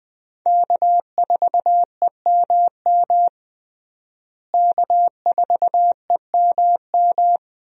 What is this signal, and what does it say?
Morse code, heard as a keyed tone, about 21 wpm, 700 Hz. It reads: K4EMM K4EMM
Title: USA Call Signs - 5 Characters - Repeated 2x - 20wpm